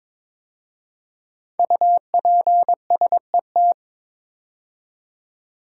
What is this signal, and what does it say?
Morse code, heard as a keyed tone, about 22 wpm, 700 Hz. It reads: UPSET
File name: 5 Letter Words - First 3 Letters Given - 22wpm